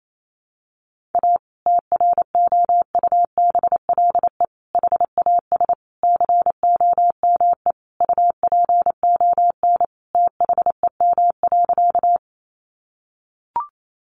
Morse code, heard as a keyed tone, about 28 wpm, 700 Hz. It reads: A TROUBLE HAS COME UPON THEM.
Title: Sentences from Top 1200 Words 28wpm